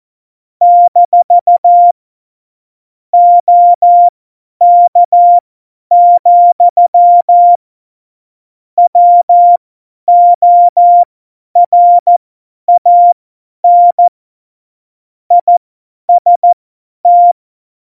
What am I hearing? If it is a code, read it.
Morse code, heard as a keyed tone, about 14 wpm, spaced as for 10 wpm, 700 Hz. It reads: - OK, WORAN IST
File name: Morsekurs / Witze / Witz 16. April 2019, 50 F70